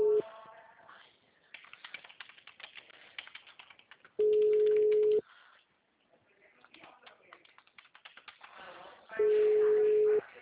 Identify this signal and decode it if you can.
Ringback tone (Europe, 425 Hz), with speech.
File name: out- -2004- - - .2616